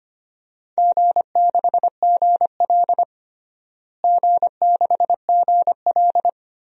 Morse code, heard as a keyed tone, about 25 wpm, 700 Hz. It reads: G6GL G6GL
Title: DXCC-100 Call Signs - 4 Characters - Repeated 2x - 25wpm